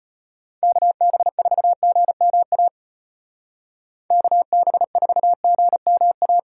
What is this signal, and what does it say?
Morse code, heard as a keyed tone, about 33 wpm, 700 Hz. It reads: KB4GMA KB4GMA